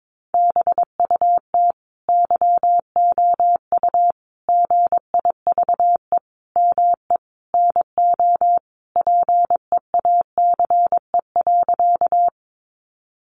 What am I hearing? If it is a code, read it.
Morse code, heard as a keyed tone, about 22 wpm, 700 Hz. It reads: BUT YOU GIVE ME NO PEACE.
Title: Sentences from Top 1500 Words 22wpm